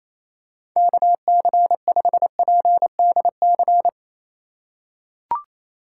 Morse code, heard as a keyed tone, about 28 wpm, 700 Hz. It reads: KC5PDC
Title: USA Call Signs - 6 Characters - 2x Character Spacing - 28wpm